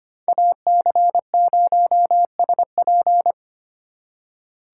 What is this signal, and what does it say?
Morse code, heard as a keyed tone, about 25 wpm, 700 Hz. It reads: AC0SP
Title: USA Call Signs - 5 Characters - 4x Character Spacing - 25wpm